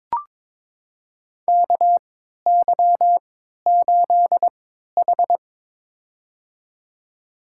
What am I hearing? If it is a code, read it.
Morse code, heard as a keyed tone, about 23 wpm, spaced as for 13 wpm, 700 Hz. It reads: KY8H
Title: USA Call Signs - 4 Characters - 3x Character Spacing - 22wpm